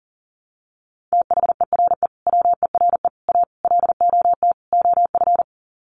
Morse code, heard as a keyed tone, about 40 wpm, 700 Hz. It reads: THERE WERE A LOT OF
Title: Sets of 5 Words 40wpm